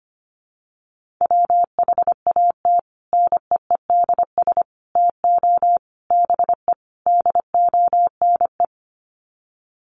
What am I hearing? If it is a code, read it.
Morse code, heard as a keyed tone, about 25 wpm, 690 Hz. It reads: WHAT NEEDS TO BE DONE
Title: Sets of 5 Words Encore 25wpm